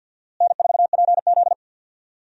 Morse code, heard as a keyed tone, about 52 wpm, 700 Hz. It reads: N4PZ
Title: USA Call Signs - 4 Characters - Repeated 2x - 50wpm